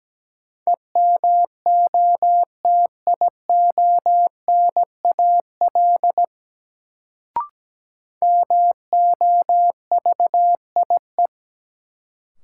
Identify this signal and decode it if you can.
Morse code, heard as a keyed tone, about 17 wpm, 700 Hz. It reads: EMOTIONAL MOVIE
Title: Sentences from Top 1000 Words - Review Part 5 - New Word Drill - 17wpm